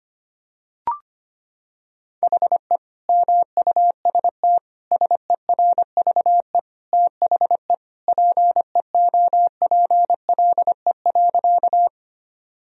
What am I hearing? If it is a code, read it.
Morse code, heard as a keyed tone, about 25 wpm, 700 Hz. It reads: HE MUST SERVE THE PEOPLE.
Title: Sentences from Top 1000 Words - Review Part 2 - 25wpm